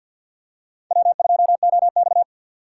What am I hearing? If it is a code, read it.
Morse code, heard as a keyed tone, about 51 wpm, 700 Hz. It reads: W2OX
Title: USA Call Signs - 4 Characters - 6x Character Spacing - 50wpm